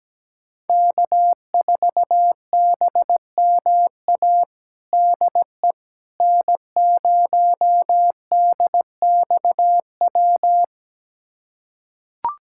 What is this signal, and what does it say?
Morse code, heard as a keyed tone, about 17 wpm, 700 Hz. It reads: K4BMA DE N0DXW
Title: USA Callsign DE Callsign - 5 Characters - 17wpm